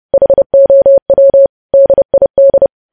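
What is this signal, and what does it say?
Morse code, heard as a keyed tone, about 30 wpm, 550 Hz. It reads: HOW DID
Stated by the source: Morse code practice of news headlines at 30 wpm, preceded by vvv.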